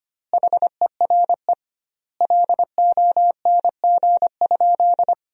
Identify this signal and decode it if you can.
Morse code, heard as a keyed tone, about 25 wpm, 700 Hz. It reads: HERE LONG?
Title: 2x Word-Spacing - Sentences from Top 200 Words - 25wpm